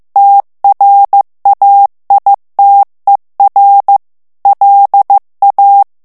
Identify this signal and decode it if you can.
Morse code, heard as a keyed tone, about 15 wpm, 800 Hz. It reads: TRAITER LA
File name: FRANCAIS TEXTE CLAIR 2 VITESSE 14 MOTS MINUTES